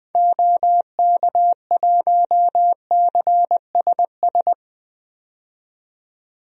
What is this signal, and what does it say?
Morse code, heard as a keyed tone, about 21 wpm, 700 Hz. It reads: OK1CSS
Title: DXCC-100 Call Signs - 6 Characters - 20wpm